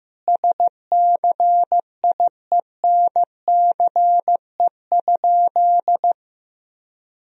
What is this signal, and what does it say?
Morse code, heard as a keyed tone, about 16 wpm, 700 Hz. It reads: SCIENCE?
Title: Sentences from Top 1000 Words - Review Part 2 - 15wpm